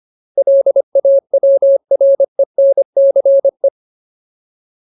Morse code, heard as a keyed tone, about 25 wpm, 550 Hz. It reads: LAWRENCE
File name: Ultimate Word List / 25WPM Ultimate